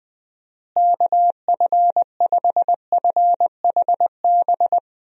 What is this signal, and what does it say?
Morse code, heard as a keyed tone, about 20 wpm, 700 Hz. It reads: KF5FHB
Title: USA Call Signs - 6 Characters - Repeated 3x - 20wpm